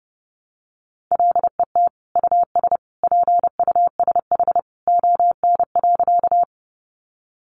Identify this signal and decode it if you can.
Morse code, heard as a keyed tone, about 30 wpm, 700 Hz. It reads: LET US PUSH ON.
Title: Sentences from Top 1300 Words 30wpm